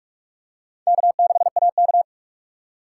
Morse code, heard as a keyed tone, about 40 wpm, 700 Hz. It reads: K6AK